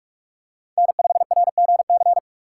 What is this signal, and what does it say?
Morse code, heard as a keyed tone, about 46 wpm, 700 Hz. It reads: N5RGC
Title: USA Call Signs - 5 Characters - 4x Character Spacing - 45wpm